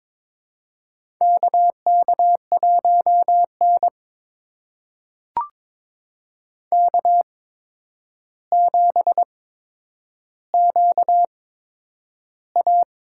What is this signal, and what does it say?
Morse code, heard as a keyed tone, about 22 wpm, spaced as for 9 wpm, 700 Hz. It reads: KK1N K7QA
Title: USA Call Signs - 4 Characters - 8x Character Spacing - 22wpm